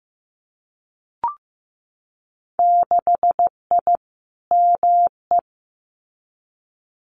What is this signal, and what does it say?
Morse code, heard as a keyed tone, about 15 wpm, 700 Hz. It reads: 6I ME
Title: Winter Field Day Exchanges Encore 15wpm